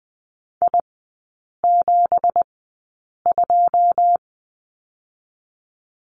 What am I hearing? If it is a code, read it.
Morse code, heard as a keyed tone, about 20 wpm, spaced as for 8 wpm, 690 Hz. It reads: I72
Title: 166 Lesson 27e - Number 7 - Three Characters 20wpm